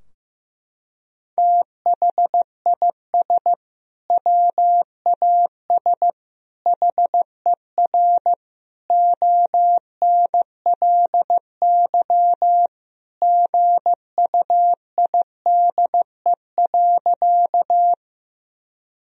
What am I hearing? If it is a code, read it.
Morse code, heard as a keyed tone, about 15 wpm, 700 Hz. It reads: THIS WAS HER ONLY GUIDE.
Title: Sentences from Top 1000 Words - Review Part 3 - 15wpm